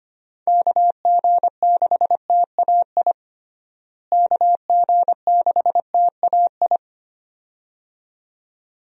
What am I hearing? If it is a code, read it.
Morse code, heard as a keyed tone, about 25 wpm, 700 Hz. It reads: KG6TAI KG6TAI
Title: USA Call Signs - 6 Characters - Repeated 2x - 25wpm